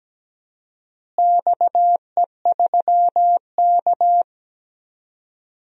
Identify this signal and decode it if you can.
Morse code, heard as a keyed tone, about 17 wpm, 700 Hz. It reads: XE3K
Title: DXCC-100 Call Signs - 4 Characters - Repeated 2x - 17wpm